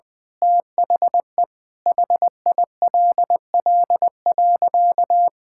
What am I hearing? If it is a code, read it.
Morse code, heard as a keyed tone, about 20 wpm, 700 Hz. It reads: THE HILL.
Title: Sentences with 6-Word Phrases - Spoken Beginning - 20wpm